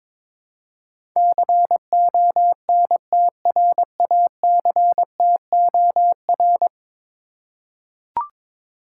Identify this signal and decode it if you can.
Morse code, heard as a keyed tone, about 22 wpm, 700 Hz. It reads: CONTRACTOR
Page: morse-code-ninja-practice-22wpm